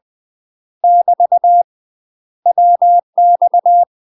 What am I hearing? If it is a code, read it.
Morse code, heard as a keyed tone, about 20 wpm, 700 Hz. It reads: <BT> WX
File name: UK MASTER QSOs TRAINING / 20WPM / UK MASTER QSOs 0001